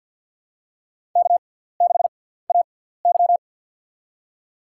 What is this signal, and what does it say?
Morse code, heard as a keyed tone, about 49 wpm, spaced as for 17 wpm, 700 Hz. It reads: K6AY
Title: USA Call Signs - 4 Characters - 6x Character Spacing - 50wpm